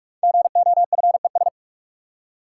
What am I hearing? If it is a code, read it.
Morse code, heard as a keyed tone, about 44 wpm, 700 Hz. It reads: GOFES
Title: Code Groups - 5 Letters - 45wpm